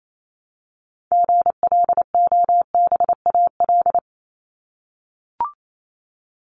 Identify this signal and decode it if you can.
Morse code, heard as a keyed tone, about 28 wpm, 700 Hz. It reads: GLOBAL
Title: Top 1001-1250 Words 28wpm